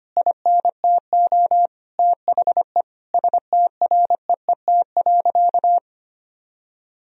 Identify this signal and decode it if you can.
Morse code, heard as a keyed tone, about 25 wpm, 700 Hz. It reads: INTO THE STREET.
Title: Sentences from Top 1000 Words - Review Part 3 - 25wpm